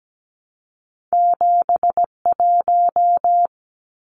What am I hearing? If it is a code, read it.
Morse code, heard as a keyed tone, about 17 wpm, 700 Hz. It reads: 71
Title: Numbers - 2-Digit 17wpm